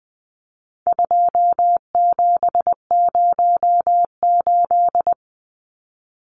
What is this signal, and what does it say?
Morse code, heard as a keyed tone, about 20 wpm, 700 Hz. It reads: 2708
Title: Numbers - 4-Digit 20wpm